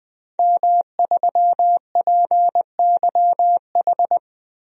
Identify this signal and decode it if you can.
Morse code, heard as a keyed tone, about 20 wpm, 700 Hz. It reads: M3PYH
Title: DXCC-100 Call Signs - 5 Characters - Repeated 3x - 20wpm